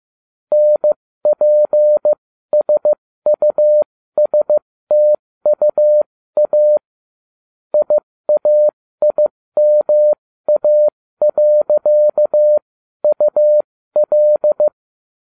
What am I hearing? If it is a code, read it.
Morse code, heard as a keyed tone, about 15 wpm, spaced as for 13 wpm, 600 Hz. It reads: NPSUSTUA IAIMA.UL